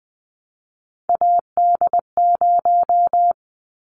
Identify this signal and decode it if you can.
Morse code, heard as a keyed tone, about 20 wpm, 700 Hz. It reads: AD0
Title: Callsign Prefixes for Top 25 DXCC - 20wpm